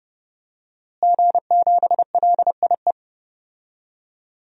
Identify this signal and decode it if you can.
Morse code, heard as a keyed tone, about 30 wpm, 700 Hz. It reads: G7LIE
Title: DXCC-100 Call Signs - 5 Characters - Repeated 2x - 30wpm